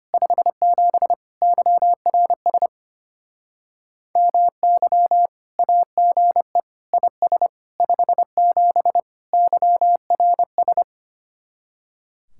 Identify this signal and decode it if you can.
Morse code, heard as a keyed tone, about 32 wpm, 700 Hz. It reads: 57 YRS MY AGE IS 57 YRS